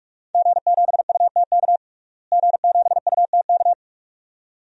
Morse code, heard as a keyed tone, about 45 wpm, 700 Hz. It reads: G7UTK G7UTK
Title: DXCC-100 Call Signs - 5 Characters - Repeated 2x - 45wpm